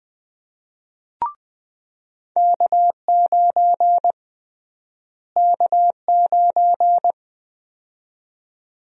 Morse code, heard as a keyed tone, about 21 wpm, 700 Hz. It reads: K9 K9
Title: USA Call Sign Prefixes - Repeated 2x - 20wpm